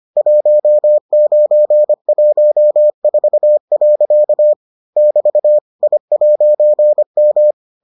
Morse code, heard as a keyed tone, about 25 wpm, 600 Hz. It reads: 1914. <BT> I'M